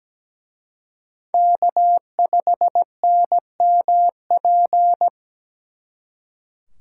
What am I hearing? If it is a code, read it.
Morse code, heard as a keyed tone, about 20 wpm, 700 Hz. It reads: K5NMP
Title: Speed Racing - USA 1x3 Call Signs 17wpm